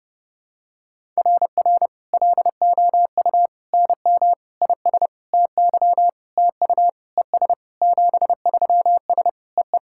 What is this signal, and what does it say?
Morse code, heard as a keyed tone, about 30 wpm, 700 Hz. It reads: RR LOU NM IS TY TU ES 73S EE
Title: SOTA QSO Practice 30wpm